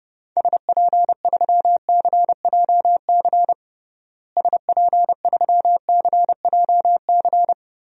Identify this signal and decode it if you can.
Morse code, heard as a keyed tone, about 30 wpm, 700 Hz. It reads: SP3CJC SP3CJC
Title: DXCC-100 Call Signs - 6 Characters - Repeated 3x - 30wpm